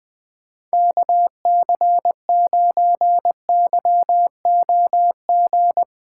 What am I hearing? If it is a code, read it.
Morse code, heard as a keyed tone, about 20 wpm, 700 Hz. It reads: KC9YOG
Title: USA Call Signs - 6 Characters - 3x Character Spacing - 20wpm